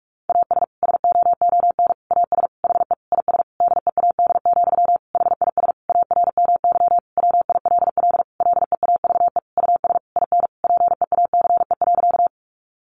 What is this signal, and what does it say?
Morse code, heard as a keyed tone, about 44 wpm, 700 Hz. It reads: AS SOON AS HE IS DEAD, HIS ARMY WILL LEAVE US IN PEACE.
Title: Sentences from Top 1500 Words 45wpm